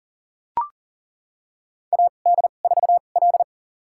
Morse code, heard as a keyed tone, about 37 wpm, spaced as for 28 wpm, 700 Hz. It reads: AD4L
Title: USA Call Signs - 4 Characters - 2x Character Spacing - 40wpm